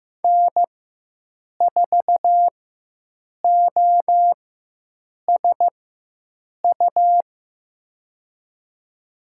Morse code, heard as a keyed tone, about 15 wpm, spaced as for 7 wpm, 700 Hz. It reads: N4OSU
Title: USA Call Signs - 5 Characters - 4x Character Spacing - 15wpm